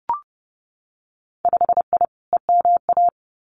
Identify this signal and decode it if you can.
Morse code, heard as a keyed tone, about 30 wpm, 700 Hz. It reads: 5I EMA
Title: Winter Field Day Exchanges Encore 30wpm